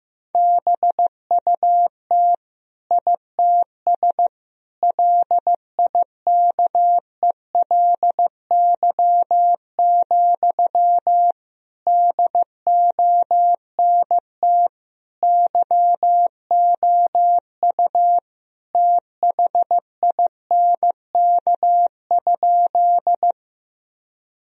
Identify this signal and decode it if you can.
Morse code, heard as a keyed tone, about 15 wpm, 700 Hz. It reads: BUT ITS LIKELY, DONT YOU THINK?
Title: Sentences from Top 1000 Words - Review Part 4 - 15wpm